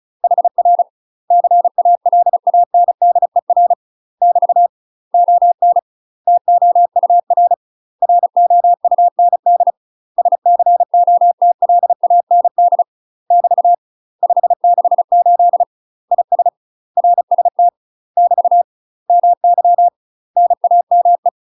Morse code, heard as a keyed tone, about 36 wpm, 700 Hz. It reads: HR CALANDER <BT> ON TOUR ROUND SCOTLAND <BT> 568 IS RST <BT> MY NAME
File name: UK MASTER QSOs TRAINING / 35WPM / UK MASTER QSOs 0014